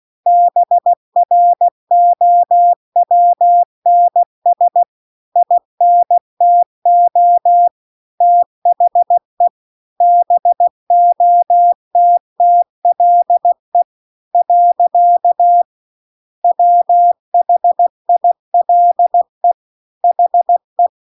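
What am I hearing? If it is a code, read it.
Morse code, heard as a keyed tone, about 16 wpm, 700 Hz. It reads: BROWNS INTO THE BOTTLE . WHILE HE
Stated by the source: Morse practice text at 16 WPM.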